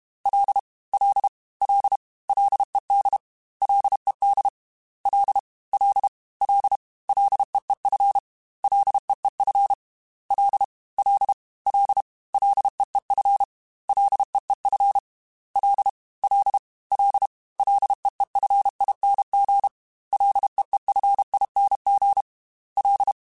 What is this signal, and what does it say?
Morse code, heard as a keyed tone, about 32 wpm, 800 Hz. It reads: L L L LED LED L L L LEEF LEEF L L L LEEF LEEF L L L LEEFING LEEFING L
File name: F and L / F and L mp3 / F and L Prefixes 3x2 Twice @30